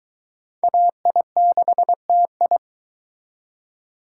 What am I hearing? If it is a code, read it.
Morse code, heard as a keyed tone, about 22 wpm, 700 Hz. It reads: AI6TI